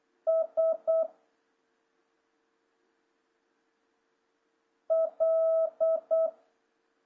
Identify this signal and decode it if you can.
Morse code, heard as a keyed tone, about 8 wpm, 650 Hz. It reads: S L
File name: jh8cbh / 2026cw